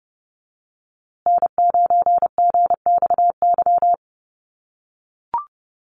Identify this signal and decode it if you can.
Morse code, heard as a keyed tone, about 30 wpm, 700 Hz. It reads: N9GXY
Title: USA Call Signs Repeated 3x 30wpm